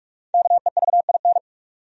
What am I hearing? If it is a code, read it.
Morse code, heard as a keyed tone, about 47 wpm, 700 Hz. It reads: KEVIN
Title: USA Common Names - Max 5 Letters - 45wpm